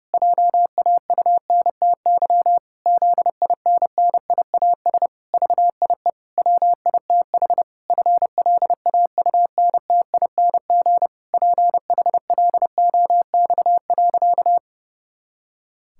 Morse code, heard as a keyed tone, about 30 wpm, 700 Hz. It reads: JAUNTY ZINNIAS VIE WITH FLAUNTING PHLOX.
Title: Pangram Challenge v2 - 30wpm